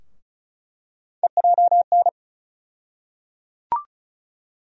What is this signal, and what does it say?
Morse code, heard as a keyed tone, about 34 wpm, 700 Hz. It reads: EJN
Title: USA Call Sign Postfixes - 3 Letters 35wpm